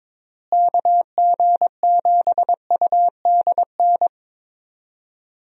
Morse code, heard as a keyed tone, about 22 wpm, 700 Hz. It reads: KG7UDN
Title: USA Call Signs - 6 Characters - Repeated 3x - 22wpm